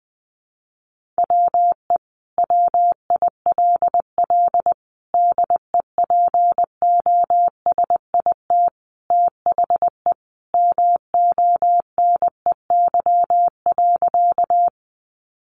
Morse code, heard as a keyed tone, about 20 wpm, 700 Hz. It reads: WE WILL DEPOSIT THE MONEY.
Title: Sentences from Top 1300 Words 20wpm